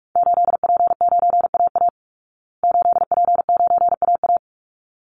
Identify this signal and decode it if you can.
Morse code, heard as a keyed tone, about 44 wpm, 700 Hz. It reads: 8P9AA 8P9AA
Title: Call Signs Repeated 3x - 45wpm